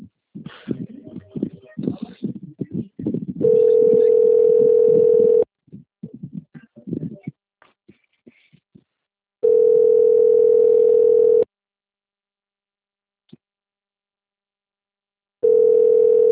Ringback tone (North America), 440 + 480 Hz.